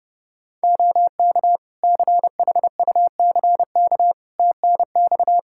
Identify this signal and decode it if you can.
Morse code, heard as a keyed tone, about 30 wpm, 700 Hz. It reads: OK CHUCK TNX
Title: Practice QSO - Exchange 4 - Ending - 30wpm